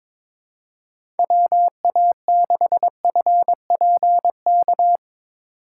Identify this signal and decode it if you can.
Morse code, heard as a keyed tone, about 22 wpm, 700 Hz. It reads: WA6FPK
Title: USA Call Signs - 6 Characters - 3x Character Spacing - 22wpm